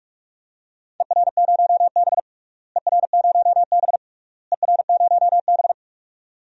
Morse code, heard as a keyed tone, about 46 wpm, 700 Hz. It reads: ER0B ER0B ER0B
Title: DXCC-100 Call Signs - 4 Characters - Repeated 3x - 45wpm